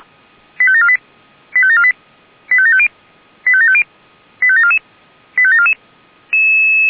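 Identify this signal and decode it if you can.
Five-tone selective call (ZVEI), heard as quick runs of five tones, tones of about 70 ms. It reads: 85748 85748 85750 85750 85744 85744
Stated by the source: ZVEI five-tone call received on SDR.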